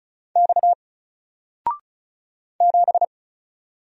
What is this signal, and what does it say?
Morse code, heard as a keyed tone, about 35 wpm, 700 Hz. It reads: X 7
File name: Single-Letter-Number-35wpm